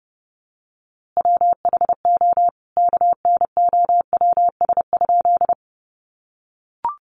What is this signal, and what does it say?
Morse code, heard as a keyed tone, about 30 wpm, 700 Hz. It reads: WHO KNOWS?
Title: A Sound Of Thunder 30wpm